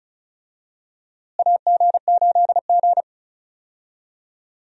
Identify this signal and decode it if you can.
Morse code, heard as a keyed tone, about 34 wpm, 700 Hz. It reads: AG8G
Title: USA Call Signs - 4 Characters - Repeated 3x - 35wpm